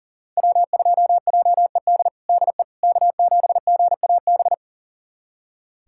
Morse code, heard as a keyed tone, about 40 wpm, 700 Hz. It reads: W2JED DE K7GAB
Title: USA Callsign DE Callsign - 5 Characters - 40wpm